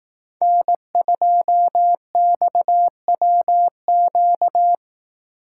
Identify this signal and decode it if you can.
Morse code, heard as a keyed tone, about 18 wpm, 700 Hz. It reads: N2XWQ
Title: Speed Racing - USA 1x3 Call Signs 15wpm